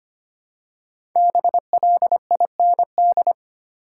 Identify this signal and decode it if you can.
Morse code, heard as a keyed tone, about 25 wpm, 700 Hz. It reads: BLIND
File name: 5 Letter Words - First 3 Letters Given - 25wpm